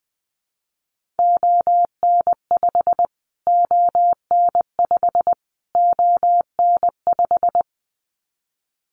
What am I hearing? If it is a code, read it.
Morse code, heard as a keyed tone, about 20 wpm, 700 Hz. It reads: ON5 ON5 ON5
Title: Callsign Prefixes for Top 25 DXCC - Repeated 3x - 20wpm